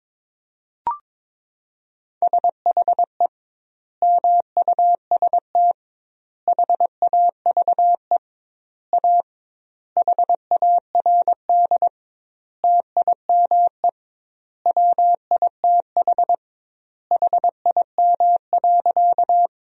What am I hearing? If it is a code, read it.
Morse code, heard as a keyed tone, about 22 wpm, 700 Hz. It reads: SHE MUST HAVE A HARD TIME WITH HIM.
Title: 2x Word-Spacing - Sentences from Top 300 Words - 22wpm